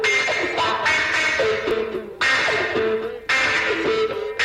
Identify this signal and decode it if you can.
Emergency Alert System (SAME) tones, heard as harsh buzzing bursts. It is heard from 0.9 to 3.6 s.
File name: wrct-log hi- 12